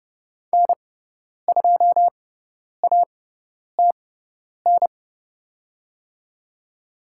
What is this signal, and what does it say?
Morse code, heard as a keyed tone, about 30 wpm, spaced as for 10 wpm, 700 Hz. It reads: N2ATN
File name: Lesson 18g - Letter m - Call Signs 30wpm